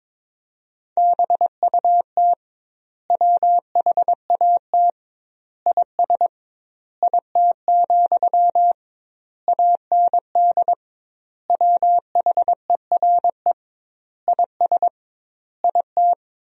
Morse code, heard as a keyed tone, about 22 wpm, 700 Hz. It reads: BUT WHAT IS IT, AND WHERE IS IT
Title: 2x Word-Spacing - Sentences from Top 300 Words - 22wpm